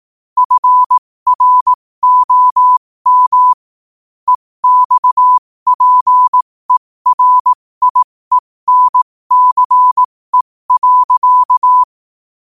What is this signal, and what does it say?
Morse code, heard as a keyed tone, about 18 wpm, spaced as for 15 wpm, 1000 Hz. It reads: FROM EXPERIENCE.
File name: qotdfiles / 15wpm